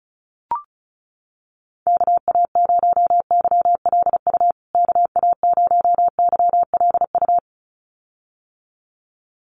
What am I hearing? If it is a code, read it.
Morse code, heard as a keyed tone, about 34 wpm, 700 Hz. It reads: KA0YLU KA0YLU
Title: USA Call Signs Repeated 2x 35wpm